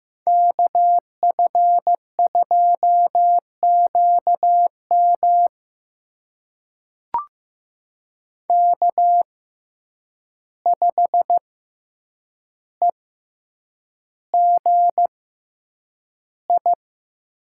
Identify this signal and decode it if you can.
Morse code, heard as a keyed tone, about 15 wpm, spaced as for 8 wpm, 700 Hz. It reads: KF2QM K5EGI
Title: USA Call Signs - 5 Characters - 6x Character Spacing - 15wpm